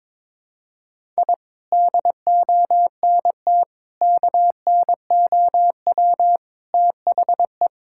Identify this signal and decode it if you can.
Morse code, heard as a keyed tone, about 22 wpm, 700 Hz. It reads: I DONT KNOW THE
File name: practice / 22wpm / Sets of 7 Words - 22wpm